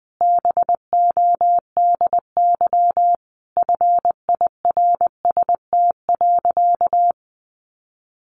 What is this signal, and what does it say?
Morse code, heard as a keyed tone, about 20 wpm, 700 Hz. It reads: BODY FIRST.
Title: Sentences from Top 1400 Words 20wpm